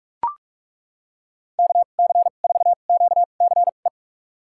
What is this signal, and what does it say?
Morse code, heard as a keyed tone, about 42 wpm, spaced as for 31 wpm, 700 Hz. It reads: KC4QCE
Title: USA Call Signs - 2x Character Spacing - 45wpm